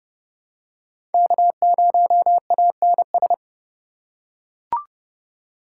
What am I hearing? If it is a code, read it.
Morse code, heard as a keyed tone, about 32 wpm, 700 Hz. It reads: K0ANS
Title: USA Call Signs - 5 Characters - 2x Character Spacing - 30wpm